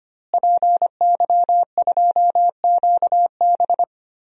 Morse code, heard as a keyed tone, about 25 wpm, 700 Hz. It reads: PY2QB
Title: DXCC-100 Call Signs - 5 Characters - Repeated 2x - 25wpm